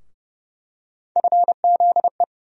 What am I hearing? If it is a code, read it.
Morse code, heard as a keyed tone, about 32 wpm, 700 Hz. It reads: FZE
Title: USA Call Sign Postfixes - 3 Letters 30wpm